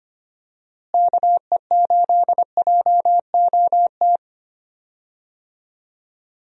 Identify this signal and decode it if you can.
Morse code, heard as a keyed tone, about 25 wpm, 700 Hz. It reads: KE8JOT
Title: USA Call Signs - 6 Characters - Repeated 3x - 25wpm